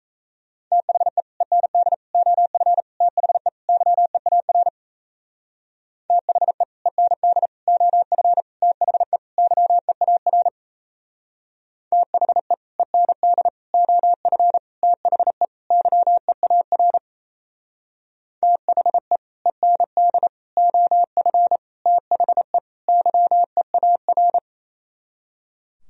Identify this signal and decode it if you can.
Morse code, heard as a keyed tone, about 35 wpm, 700 Hz. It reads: THE END OF THE YEAR THE END OF THE YEAR THE END OF THE YEAR THE END OF THE YEAR
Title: Speed-Racing - Sets of 5 Words 28wpm